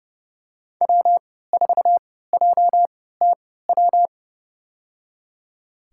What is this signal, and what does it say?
Morse code, heard as a keyed tone, about 30 wpm, spaced as for 17 wpm, 700 Hz. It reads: W4JTW
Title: USA Call Signs - 5 Characters - 3x Character Spacing - 30wpm